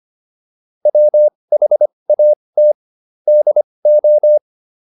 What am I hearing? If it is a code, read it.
Morse code, heard as a keyed tone, about 25 wpm, spaced as for 20 wpm, 600 Hz. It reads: WHAT DO